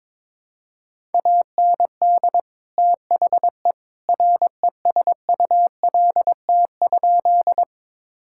Morse code, heard as a keyed tone, about 22 wpm, 700 Hz. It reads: AND THE RESULT?
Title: Sentences from Top 1000 Words - Review Part 5 - 22wpm